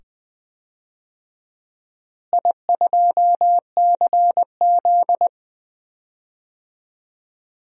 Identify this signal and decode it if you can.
Morse code, heard as a keyed tone, about 21 wpm, 700 Hz. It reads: I2CZ
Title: DXCC-100 Call Signs - 4 Characters - Repeated 2x - 20wpm